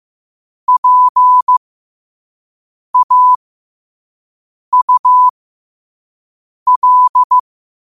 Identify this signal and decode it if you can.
Morse code, heard as a keyed tone, about 15 wpm, spaced as for 5 wpm, 1000 Hz. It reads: PAUL
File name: qotdfiles / 5wpm